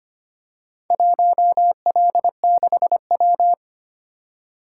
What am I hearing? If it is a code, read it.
Morse code, heard as a keyed tone, about 25 wpm, 700 Hz. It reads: 1L6W
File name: Four Letter-Number Combinations - 25wpm